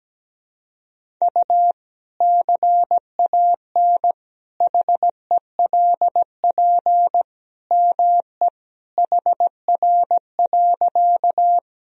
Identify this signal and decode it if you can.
Morse code, heard as a keyed tone, about 17 wpm, 700 Hz. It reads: U CAN HELP ME HR.
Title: Sentences from Top 200 Words and Abbreviations 17wpm